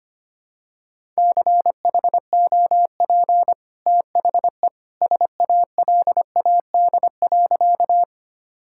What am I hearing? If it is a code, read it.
Morse code, heard as a keyed tone, about 25 wpm, 700 Hz. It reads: CHOP THE SALAD.
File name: Easy Sentences with Hints - 25wpm